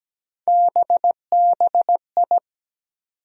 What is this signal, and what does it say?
Morse code, heard as a keyed tone, about 17 wpm, 700 Hz. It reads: BBI
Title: USA Call Sign Postfixes - 3 Letters 17wpm